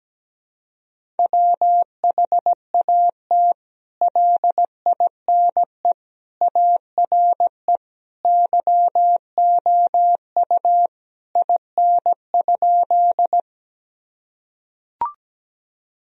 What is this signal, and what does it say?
Morse code, heard as a keyed tone, about 17 wpm, 700 Hz. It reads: WHAT LINE ARE YOU IN?
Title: Sentences from Top 1000 Words - Review Part 1 - 17wpm